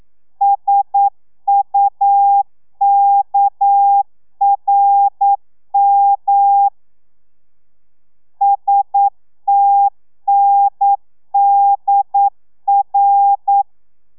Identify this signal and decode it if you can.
Morse code, heard as a keyed tone, about 9 wpm, 800 Hz. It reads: SUKRM STNDR
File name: telegrafia / i2tfj / Lez6 40